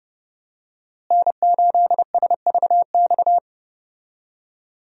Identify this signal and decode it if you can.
Morse code, heard as a keyed tone, about 30 wpm, 700 Hz. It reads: N8SVX
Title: USA Call Signs - 5 Characters - Repeated 2x - 30wpm